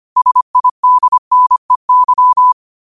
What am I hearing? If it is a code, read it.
Morse code, heard as a keyed tone, about 25 wpm, 1000 Hz. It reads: SIDNEY
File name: qotdfiles / 25wpm